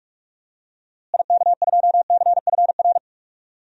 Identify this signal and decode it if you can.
Morse code, heard as a keyed tone, about 41 wpm, 700 Hz. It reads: IK2CFR
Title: DXCC-100 Call Signs - 6 Characters - Repeated 2x - 45wpm